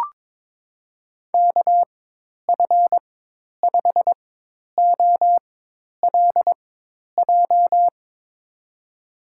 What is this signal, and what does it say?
Morse code, heard as a keyed tone, about 22 wpm, spaced as for 10 wpm, 700 Hz. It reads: KF5OLJ
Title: USA Call Signs - 6 Characters - 4x Character Spacing - 22wpm